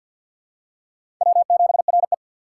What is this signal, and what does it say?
Morse code, heard as a keyed tone, about 53 wpm, 700 Hz. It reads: W7RE
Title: USA Call Signs - 4 Characters - 6x Character Spacing - 50wpm